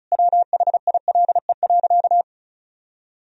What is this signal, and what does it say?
Morse code, heard as a keyed tone, about 34 wpm, 700 Hz. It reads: WHILE.
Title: Sentences from Top 1000 Words - Review Part 1 - 35wpm